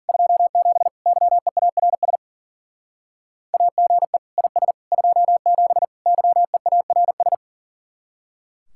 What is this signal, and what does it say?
Morse code, heard as a keyed tone, about 48 wpm, 700 Hz. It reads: 27 YEARS AGE IS 27 YEARS
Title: Speed-Racing Common QSO Phrases - 40wpm